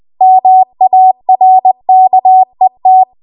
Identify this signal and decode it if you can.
Morse code, heard as a keyed tone, about 20 wpm, 750 Hz. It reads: MARKET